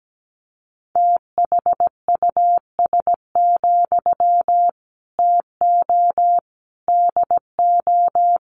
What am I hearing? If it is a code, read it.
Morse code, heard as a keyed tone, about 17 wpm, 700 Hz. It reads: THUS, TO DO
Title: Sentences from Top 900 Words 17wpm